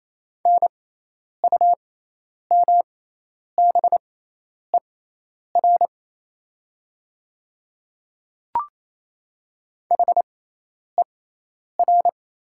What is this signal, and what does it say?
Morse code, heard as a keyed tone, about 28 wpm, spaced as for 10 wpm, 700 Hz. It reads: NUMBER HER
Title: 6x Character Spacing - Top 100 Words - 28wpm